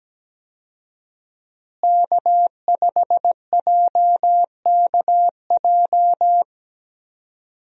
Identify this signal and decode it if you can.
Morse code, heard as a keyed tone, about 17 wpm, 700 Hz. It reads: K5JKJ
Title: USA Call Signs - 5 Characters - Repeated 3x - 17wpm